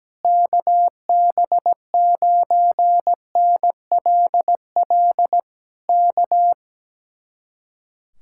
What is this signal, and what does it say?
Morse code, heard as a keyed tone, about 17 wpm, 700 Hz. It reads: KB9NLL K
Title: Speed-Racing Common QSO Phrases - 17wpm